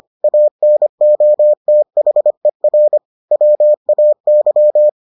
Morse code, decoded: ANOTHER WAY

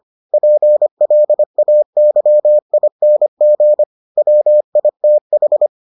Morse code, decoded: PLAYING WITH

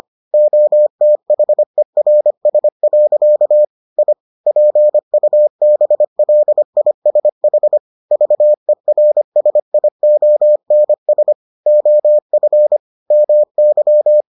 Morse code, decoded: OTHERS. I PUBLISH VERSIONS OF MY

